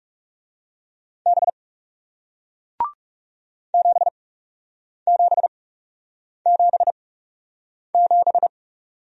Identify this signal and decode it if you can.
Morse code, heard as a keyed tone, about 41 wpm, spaced as for 8 wpm, 700 Hz. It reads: B 7777